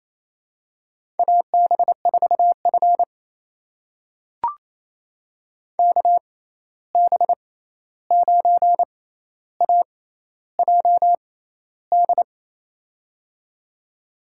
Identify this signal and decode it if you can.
Morse code, heard as a keyed tone, about 28 wpm, spaced as for 13 wpm, 700 Hz. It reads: AB4F KB9AJD
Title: USA Call Signs - 6x Character Spacing - 28wpm